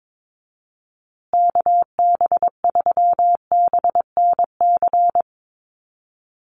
Morse code, heard as a keyed tone, about 22 wpm, 700 Hz. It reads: KB3BNC
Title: USA Call Signs Repeated 3x 22wpm